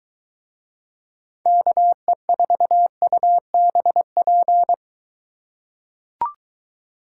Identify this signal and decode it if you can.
Morse code, heard as a keyed tone, about 22 wpm, 700 Hz. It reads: KE4UBP